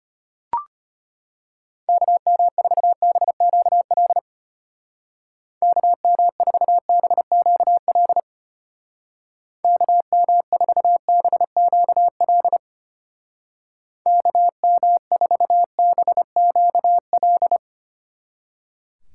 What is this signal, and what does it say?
Morse code, heard as a keyed tone, about 32 wpm, 700 Hz. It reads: KM4BQL KM4BQL KM4BQL KM4BQL